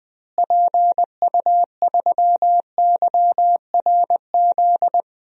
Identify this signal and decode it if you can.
Morse code, heard as a keyed tone, about 20 wpm, 700 Hz. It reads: PU3YRZ
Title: DXCC-100 Call Signs - 6 Characters - Repeated 3x - 20wpm